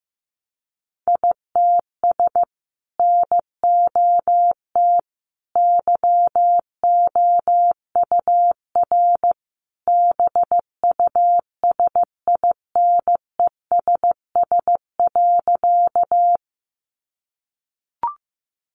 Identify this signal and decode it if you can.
Morse code, heard as a keyed tone, about 15 wpm, 700 Hz. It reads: ITS NOT YOUR BUSINESS.
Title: Sentences from Top 600 Words 15wpm